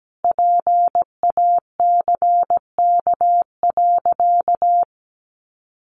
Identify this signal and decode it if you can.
Morse code, heard as a keyed tone, about 17 wpm, 700 Hz. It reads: PACK.
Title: Sentences from Top 1300 Words 17wpm